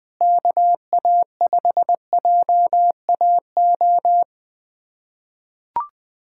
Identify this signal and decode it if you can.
Morse code, heard as a keyed tone, about 20 wpm, 700 Hz. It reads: KA5JAO
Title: USA Callsign DE Callsign - 6 Characters - 20wpm